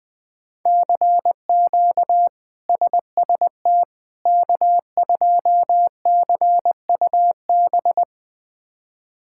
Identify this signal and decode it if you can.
Morse code, heard as a keyed tone, about 20 wpm, 700 Hz. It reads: CQ SST K2CUB
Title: K1USN SST QSO Practice - 20wpm